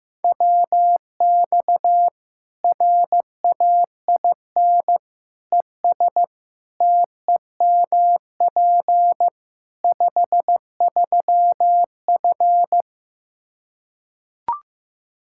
Morse code, heard as a keyed tone, about 15 wpm, 700 Hz. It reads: WX RAIN ES TEMP 53F
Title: Practice QSO - Exchange 2 - Info - 15wpm